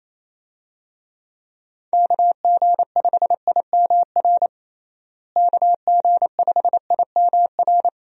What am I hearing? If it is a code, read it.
Morse code, heard as a keyed tone, about 27 wpm, 700 Hz. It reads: KG5IMR KG5IMR
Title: USA Call Signs - 6 Characters - Repeated 2x - 28wpm